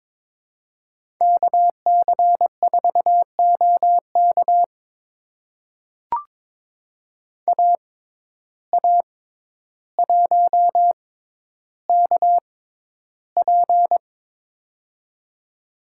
Morse code, heard as a keyed tone, about 22 wpm, spaced as for 11 wpm, 700 Hz. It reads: KC4OK AA1KP
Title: USA Call Signs - 5 Characters - 6x Character Spacing - 22wpm